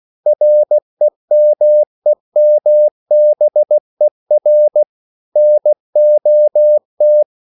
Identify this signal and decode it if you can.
Morse code, heard as a keyed tone, about 16 wpm, 600 Hz. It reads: REMEMBER NOT
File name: CW / Practice / EnglishQuotations0116